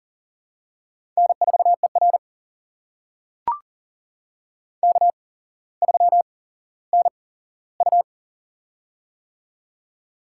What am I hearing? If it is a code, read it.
Morse code, heard as a keyed tone, about 40 wpm, spaced as for 17 wpm, 700 Hz. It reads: N4ER K3NU
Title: USA Call Signs - 4 Characters - 8x Character Spacing - 40wpm